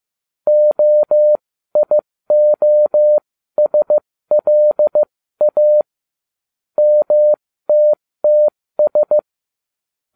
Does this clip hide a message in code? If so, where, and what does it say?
Yes: Morse code from 0.5 to 9.2 s, heard as a keyed tone, about 15 wpm, spaced as for 13 wpm, 600 Hz. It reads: OIOSLA MTTS